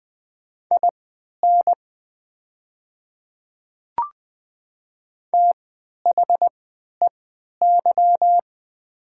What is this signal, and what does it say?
Morse code, heard as a keyed tone, about 20 wpm, spaced as for 11 wpm, 700 Hz. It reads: IN THEY
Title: 3x Character Spacing - Top 100 Words - 20wpm